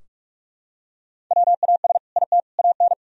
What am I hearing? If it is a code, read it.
Morse code, heard as a keyed tone, about 44 wpm, 700 Hz. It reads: WAS IT AN
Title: Sentences from Top 1000 Words - Review Part 4 - 45wpm